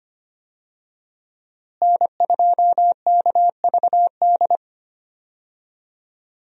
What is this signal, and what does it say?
Morse code, heard as a keyed tone, about 25 wpm, 700 Hz. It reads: N2KVD
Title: USA Call Signs - 5 Characters - Repeated 2x - 25wpm